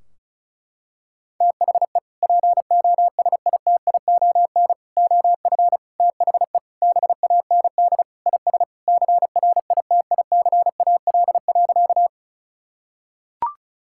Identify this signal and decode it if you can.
Morse code, heard as a keyed tone, about 36 wpm, 700 Hz. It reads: THE POSITION OF THE BAND IS CRITICAL.
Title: Sentences from Top 1600 Words 35wpm